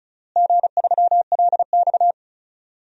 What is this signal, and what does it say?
Morse code, heard as a keyed tone, about 36 wpm, 700 Hz. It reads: G3LX
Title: DXCC-100 Call Signs - 4 Characters - Repeated 3x - 35wpm